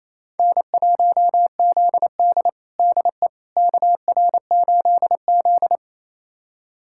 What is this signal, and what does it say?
Morse code, heard as a keyed tone, about 28 wpm, 700 Hz. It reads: N1ZD DE KR8Z
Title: USA Callsign DE Callsign - 4 Characters - 28wpm